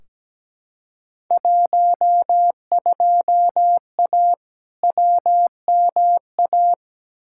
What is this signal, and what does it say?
Morse code, heard as a keyed tone, about 17 wpm, 700 Hz. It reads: 12A WMA
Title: Field Day Exchanges Encore - 17wpm